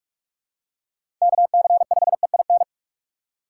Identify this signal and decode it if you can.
Morse code, heard as a keyed tone, about 48 wpm, 700 Hz. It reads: KC5EIN